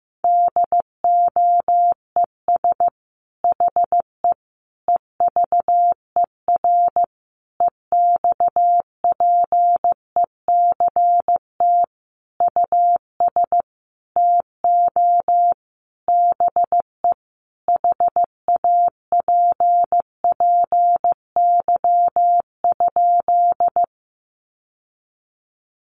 Morse code, heard as a keyed tone, about 15 wpm, 700 Hz. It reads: DOES HE EVER EXPECT US TO BE HAPPY?
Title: Sentences from Top 1100 Words 15wpm